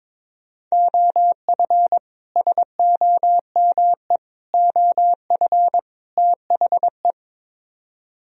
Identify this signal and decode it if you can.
Morse code, heard as a keyed tone, about 22 wpm, 700 Hz. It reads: OF SOME OF THE
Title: Speed-Racing - Sets of 4 Words 22wpm